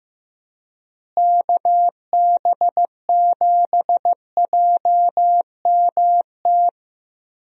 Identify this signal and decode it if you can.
Morse code, heard as a keyed tone, about 15 wpm, 700 Hz. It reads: KB7JMT
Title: USA Call Signs - 6 Characters - Repeated 3x - 15wpm